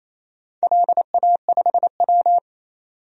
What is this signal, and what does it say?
Morse code, heard as a keyed tone, about 27 wpm, 700 Hz. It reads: LA5W
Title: DXCC-100 Call Signs - 4 Characters - 28wpm